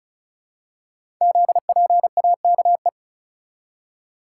Morse code, heard as a keyed tone, about 36 wpm, 690 Hz. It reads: ZPAKE
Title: Code Groups - 5 Letter:Numbers - 35wpm